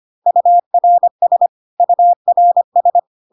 Morse code, heard as a keyed tone, about 25 wpm, 700 Hz. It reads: URS URS
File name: CW abbreviations alphabetical repeated 3 times 25 25wpm ik1hge